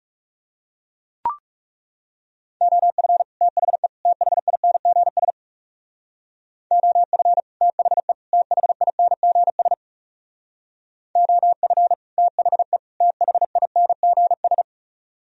Morse code, heard as a keyed tone, about 35 wpm, 700 Hz. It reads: OF THE THINGS OF THE THINGS OF THE THINGS
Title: Speed-Racing - Sets of 3 Words 30wpm